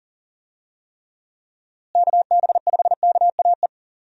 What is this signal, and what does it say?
Morse code, heard as a keyed tone, about 40 wpm, 700 Hz. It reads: KB5KAE
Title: USA Call Signs - 6 Characters - Repeated 2x - 40wpm